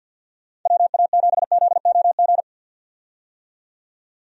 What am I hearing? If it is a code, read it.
Morse code, heard as a keyed tone, about 51 wpm, 700 Hz. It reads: WA7ZOG
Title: USA Call Signs - 6 Characters - 50wpm